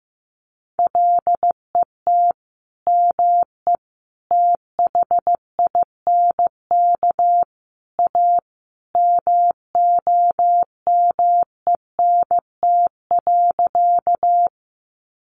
Morse code, heard as a keyed tone, about 15 wpm, 700 Hz. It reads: LET ME THINK A MOMENT.
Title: Sentences from Top 1000 Words 15wpm